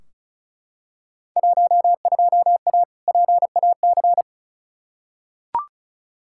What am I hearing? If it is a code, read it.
Morse code, heard as a keyed tone, about 34 wpm, 700 Hz. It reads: 12A PAC
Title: Field Day Exchanges Encore - 35wpm